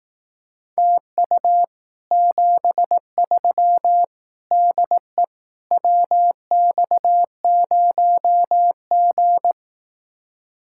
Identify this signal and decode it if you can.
Morse code, heard as a keyed tone, about 18 wpm, 700 Hz. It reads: TU 73 DE WX0G
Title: Speed-Racing Common QSO Phrases Encore - 15wpm